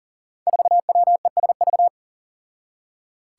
Morse code, heard as a keyed tone, about 38 wpm, 700 Hz. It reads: 4WESV
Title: Five Challenging Letter-Number Combinations 40wpm